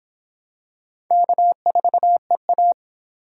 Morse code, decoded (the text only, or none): K4EA